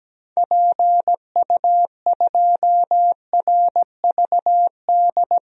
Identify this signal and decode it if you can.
Morse code, heard as a keyed tone, about 17 wpm, 700 Hz. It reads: PU2RVD
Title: DXCC-100 Call Signs - 6 Characters - Repeated 2x - 17wpm